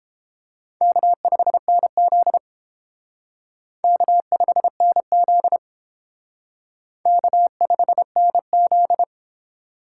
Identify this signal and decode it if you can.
Morse code, heard as a keyed tone, about 29 wpm, 700 Hz. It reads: K5NZ K5NZ K5NZ